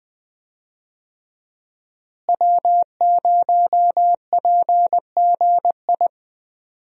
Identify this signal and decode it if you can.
Morse code, heard as a keyed tone, about 20 wpm, 700 Hz. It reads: W0PGI
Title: USA Call Signs - 5 Characters - Repeated 3x - 20wpm